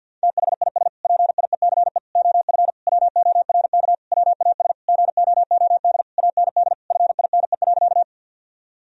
Morse code, heard as a keyed tone, about 50 wpm, 700 Hz. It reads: THIS PIECE OF WORK WAS GOOD AND FINE.
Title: Sentences from Top 1000 Words - Review Part 2 - 50wpm